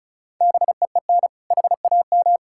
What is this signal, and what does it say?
Morse code, heard as a keyed tone, about 36 wpm, 700 Hz. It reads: BEEN HAM
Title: Practice QSO - Exchange 3 - More Info - 35wpm